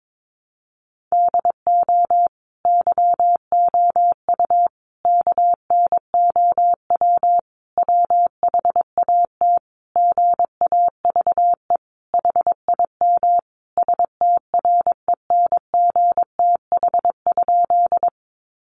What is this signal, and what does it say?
Morse code, heard as a keyed tone, about 22 wpm, 700 Hz. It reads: DO YOU KNOW WHAT GAVE HIM STRENGTH?
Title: Sentences from Top 1300 Words 22wpm